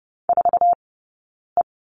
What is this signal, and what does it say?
Morse code, heard as a keyed tone, about 30 wpm, 700 Hz. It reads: 4 E
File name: Lesson 20d - Number 3 - Two Characters 30wpm